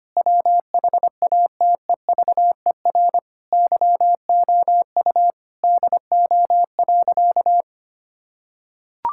Morse code, heard as a keyed tone, about 25 wpm, 700 Hz. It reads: WHATEVER YOU DO.
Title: Sentences from Top 1600 Words 25wpm